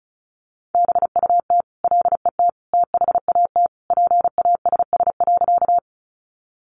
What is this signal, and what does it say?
Morse code, heard as a keyed tone, about 34 wpm, 700 Hz. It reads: BUT LET THAT PASS.
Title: Sentences from Top 400 Words 35wpm